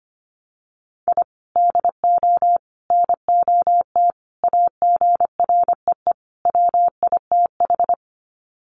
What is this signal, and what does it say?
Morse code, heard as a keyed tone, about 25 wpm, 700 Hz. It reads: I DO NOT AGREE WITH
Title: Sets of 5 Words Encore 25wpm